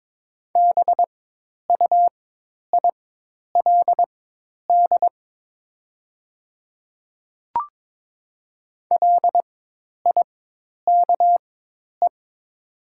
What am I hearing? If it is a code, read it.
Morse code, heard as a keyed tone, about 22 wpm, spaced as for 10 wpm, 700 Hz. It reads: BUILD LIKE
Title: 4x Character Spacing - Top 200 Words - 22wpm